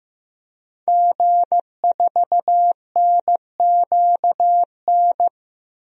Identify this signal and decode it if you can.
Morse code, heard as a keyed tone, about 15 wpm, 700 Hz. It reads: G4NQN
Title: DXCC-100 Call Signs - 5 Characters - Repeated 3x - 15wpm